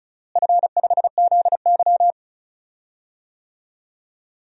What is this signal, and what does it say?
Morse code, heard as a keyed tone, about 34 wpm, 700 Hz. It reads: F5ZY
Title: DXCC-100 Call Signs - 4 Characters - Repeated 2x - 35wpm